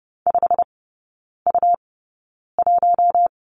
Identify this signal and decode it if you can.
Morse code, heard as a keyed tone, about 30 wpm, spaced as for 9 wpm, 700 Hz. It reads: 5U1
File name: Lesson 18e - Letter m - Three Characters 30wpm